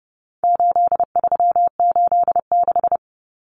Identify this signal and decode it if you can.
Morse code, heard as a keyed tone, about 30 wpm, 700 Hz. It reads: 8386
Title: Numbers - 4-Digit 30wpm